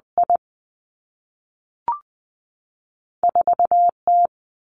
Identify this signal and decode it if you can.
Morse code, heard as a keyed tone, about 20 wpm, 700 Hz. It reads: I 4 T
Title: Two Challenging Letter-Number Combinations 20wpm